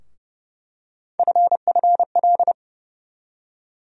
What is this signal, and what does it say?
Morse code, heard as a keyed tone, about 32 wpm, 700 Hz. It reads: FFL
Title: USA Call Sign Postfixes - 3 Letters 30wpm